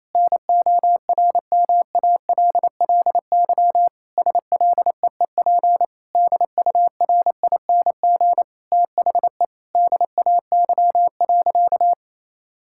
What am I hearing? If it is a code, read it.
Morse code, heard as a keyed tone, about 28 wpm, 700 Hz. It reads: NORMALLY SLEEP DURING THE DAY.
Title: Sentences from Top 1000 Words - Review Part 6 - 28wpm